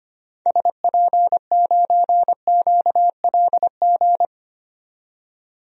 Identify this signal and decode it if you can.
Morse code, heard as a keyed tone, about 25 wpm, 700 Hz. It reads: SP9QLG
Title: DXCC-100 Call Signs - 6 Characters - 25wpm